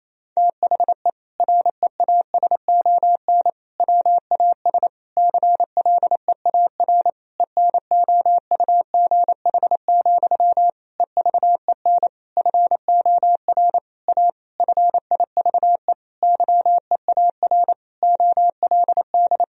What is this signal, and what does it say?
Morse code, heard as a keyed tone, about 28 wpm, 700 Hz. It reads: THE REASON WAS CLEAR ENOUGH, EVEN FOR A FIVE YEAR OLD
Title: Sentences from Top 1000 Words - Review Part 2 - 28wpm